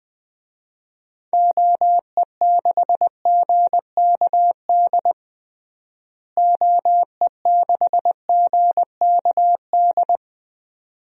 Morse code, heard as a keyed tone, about 21 wpm, 700 Hz. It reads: OE6GKD OE6GKD
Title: DXCC-100 Call Signs - 6 Characters - Repeated 2x - 20wpm